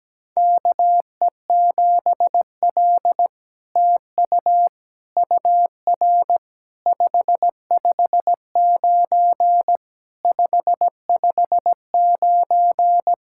Morse code, heard as a keyed tone, about 17 wpm, 700 Hz. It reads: KE7L TU UR 559 559
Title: SOTA QSO Practice 17wpm